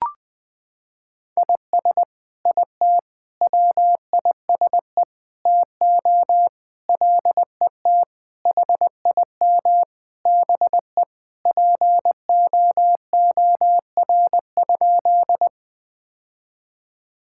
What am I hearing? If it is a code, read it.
Morse code, heard as a keyed tone, about 20 wpm, 700 Hz. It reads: IS IT WISE TO LET HIM BE POOR?
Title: Sentences from Top 1600 Words 20wpm